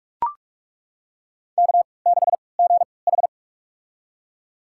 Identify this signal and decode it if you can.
Morse code, heard as a keyed tone, about 42 wpm, spaced as for 25 wpm, 700 Hz. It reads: K6GH
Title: USA Call Signs - 4 Characters - 3x Character Spacing - 45wpm